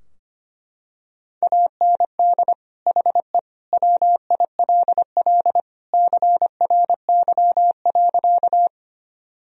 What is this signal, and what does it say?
Morse code, heard as a keyed tone, about 25 wpm, 700 Hz. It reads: AND HE WILL CRY.
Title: Sentences from Top 1000 Words - Review Part 2 - 25wpm